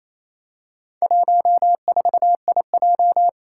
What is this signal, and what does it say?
Morse code, heard as a keyed tone, about 27 wpm, 700 Hz. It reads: 14IJ
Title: Four Challenging Letter-Number Combinations 28wpm